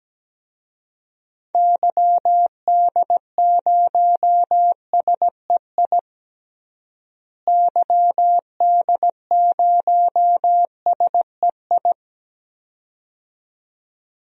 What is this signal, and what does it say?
Morse code, heard as a keyed tone, about 17 wpm, 700 Hz. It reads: YD0SEI YD0SEI
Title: DXCC-100 Call Signs - 6 Characters - Repeated 2x - 17wpm